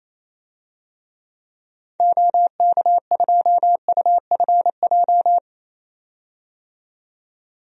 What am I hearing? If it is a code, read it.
Morse code, heard as a keyed tone, about 28 wpm, 700 Hz. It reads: OK2UFJ
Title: DXCC-100 Call Signs - 6 Characters - 28wpm